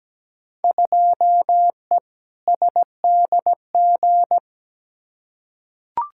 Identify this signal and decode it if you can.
Morse code, heard as a keyed tone, about 17 wpm, 700 Hz. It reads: 2E SDG
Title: Field Day Exchanges - 17wpm